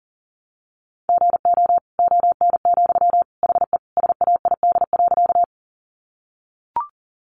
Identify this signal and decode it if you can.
Morse code, heard as a keyed tone, about 40 wpm, 700 Hz. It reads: GO ON, HE SAID.
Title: Sentences from Top 100 Words 40wpm